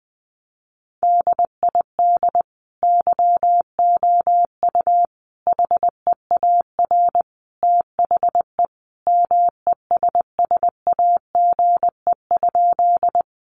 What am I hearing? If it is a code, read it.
Morse code, heard as a keyed tone, about 20 wpm, 700 Hz. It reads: DID YOU HEAR THE MESSAGE?
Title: Sentences from Top 1100 Words 20wpm